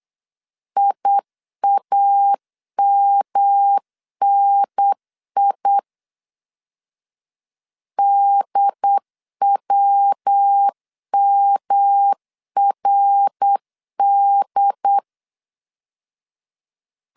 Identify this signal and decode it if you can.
Morse code, heard as a keyed tone, about 8 wpm, 800 Hz. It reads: IAMNI DWMRD
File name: CW / DET / 1 3 1500C S B 8wpm 4141